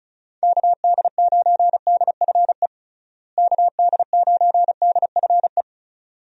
Morse code, heard as a keyed tone, about 36 wpm, 700 Hz. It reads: KD9DFE KD9DFE
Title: USA Call Signs - 6 Characters - Repeated 2x - 35wpm